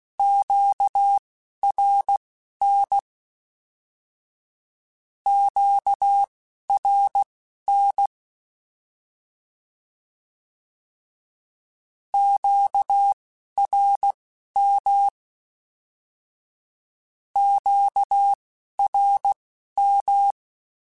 Morse code, heard as a keyed tone, about 16 wpm, spaced as for 11 wpm, 800 Hz. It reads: QRN QRN QRM QRM
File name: Q Calls / Q Calls-mp3s / Q Calls Twice @10